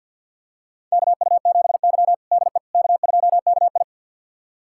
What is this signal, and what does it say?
Morse code, heard as a keyed tone, about 51 wpm, 700 Hz. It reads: KU7Y DE K2KI
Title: USA Callsign DE Callsign - 4 Characters - 50wpm